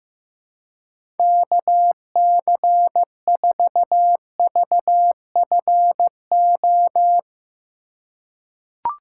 Morse code, heard as a keyed tone, about 15 wpm, 700 Hz. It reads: KC4VFO
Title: USA Call Signs - 6 Characters - 2x Character Spacing - 15wpm